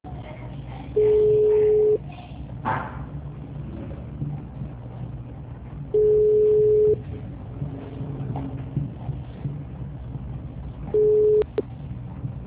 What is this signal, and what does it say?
Ringback tone (Europe, 425 Hz), with speech.